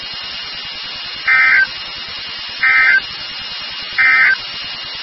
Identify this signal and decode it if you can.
Emergency Alert System (SAME) tones, heard as harsh buzzing bursts. It is heard from 1.2 to 4.4 s.
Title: NOAA Weather Radio WXK36